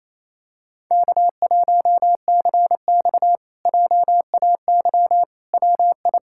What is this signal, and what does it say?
Morse code, heard as a keyed tone, about 28 wpm, 700 Hz. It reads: K1CX JAY WI
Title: K1USN SST QSO Practice - 28wpm